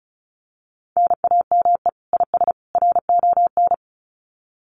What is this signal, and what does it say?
Morse code, heard as a keyed tone, about 34 wpm, 700 Hz. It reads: NAME IS RON
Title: Common QSO Phrases 35wpm